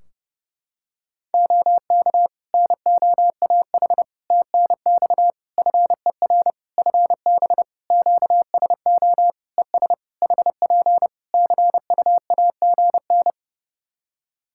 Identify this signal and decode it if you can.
Morse code, heard as a keyed tone, about 31 wpm, 700 Hz. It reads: OK NOAH TNX FER FB QSO ES HP CUAGN
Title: Practice QSO - Exchange 4 - Ending - 30wpm